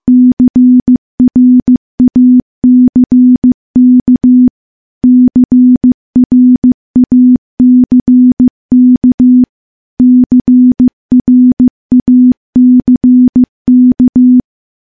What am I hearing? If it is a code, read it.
Morse code, heard as a keyed tone, about 16 wpm, 250 Hz. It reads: CRACK CRACK CRACK
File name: morse-training-15WPM